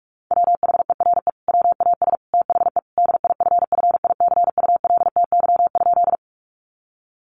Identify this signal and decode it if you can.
Morse code, heard as a keyed tone, about 44 wpm, 700 Hz. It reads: WHERE WAS THE DIFFICULTY?
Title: Sentences from Top 1500 Words 45wpm